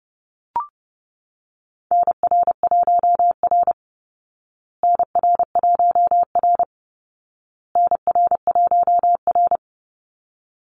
Morse code, heard as a keyed tone, about 30 wpm, 700 Hz. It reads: NR1R NR1R NR1R
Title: Call Signs Repeated 2x - 30wpm